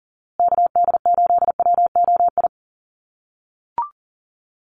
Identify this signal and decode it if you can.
Morse code, heard as a keyed tone, about 40 wpm, 700 Hz. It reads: KD8WOI
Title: USA Call Signs Repeated 2x 40wpm